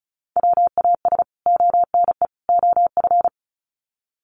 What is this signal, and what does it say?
Morse code, heard as a keyed tone, about 34 wpm, 700 Hz. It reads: WAS ONE OF
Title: Sets of 3 Words 35wpm